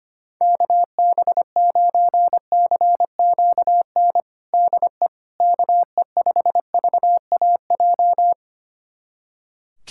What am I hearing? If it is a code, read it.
Morse code, heard as a keyed tone, about 25 wpm, 700 Hz. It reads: KB9CQN DE KE5VAJ
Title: USA Callsign DE Callsign - 6 Characters - 25wpm